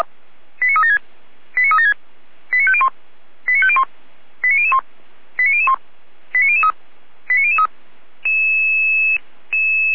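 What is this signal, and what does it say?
Five-tone selective call (ZVEI), heard as quick runs of five tones, tones of about 70 ms. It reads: 89376 89376 89501 89501 89001 89001 89003 89003
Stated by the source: ZVEI five-tone selective call 89160, received on SDR, fire brigade Orlamünde (feuerwehr-orlamuende).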